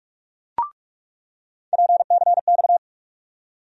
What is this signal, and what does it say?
Morse code, heard as a keyed tone, about 42 wpm, 700 Hz. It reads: PCX
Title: USA Call Sign Postfixes - 3 Letters 45wpm